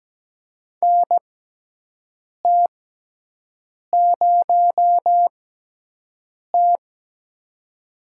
Morse code, heard as a keyed tone, about 17 wpm, spaced as for 6 wpm, 700 Hz. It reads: NT0T
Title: USA Call Signs - 4 Characters - 6x Character Spacing - 17wpm